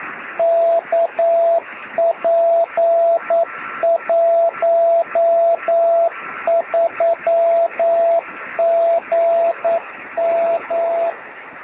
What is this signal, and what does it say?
Morse code, heard as a keyed tone, about 9 wpm, 670 Hz, with radio noise. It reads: KP13GM
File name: sm7lcb / 2109z oh6shf 23cm beacon